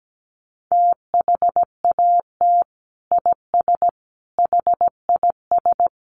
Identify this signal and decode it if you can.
Morse code, heard as a keyed tone, about 17 wpm, 700 Hz. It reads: THAT IS HIS
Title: Sentences from Top 900 Words 17wpm